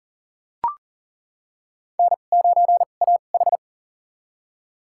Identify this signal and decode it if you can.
Morse code, heard as a keyed tone, about 37 wpm, spaced as for 28 wpm, 700 Hz. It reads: N9AH
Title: USA Call Signs - 4 Characters - 2x Character Spacing - 40wpm